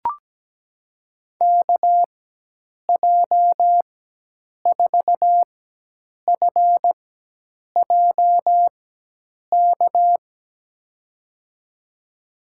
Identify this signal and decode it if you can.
Morse code, heard as a keyed tone, about 17 wpm, spaced as for 8 wpm, 700 Hz. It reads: KJ4FJK
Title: USA Call Signs - 6 Characters - 4x Character Spacing - 17wpm